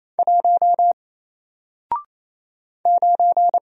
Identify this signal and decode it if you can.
Morse code, heard as a keyed tone, about 27 wpm, 700 Hz. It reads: 1 9
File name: Single-Letter-Number-28wpm